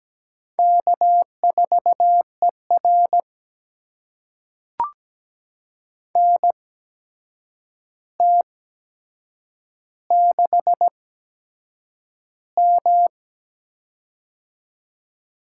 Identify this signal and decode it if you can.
Morse code, heard as a keyed tone, about 17 wpm, spaced as for 7 wpm, 700 Hz. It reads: K4ER NT6M
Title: USA Call Signs - 4 Characters - 8x Character Spacing - 17wpm